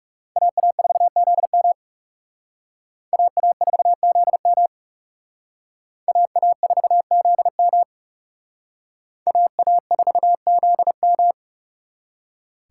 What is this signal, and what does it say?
Morse code, heard as a keyed tone, about 37 wpm, 700 Hz. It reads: AA4ZM AA4ZM AA4ZM AA4ZM
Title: Speed Racing - USA Call Signs 30wpm